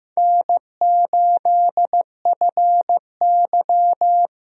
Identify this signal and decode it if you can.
Morse code, heard as a keyed tone, about 15 wpm, 700 Hz. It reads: N8FY
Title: USA Call Signs - 4 Characters - Repeated 3x - 15wpm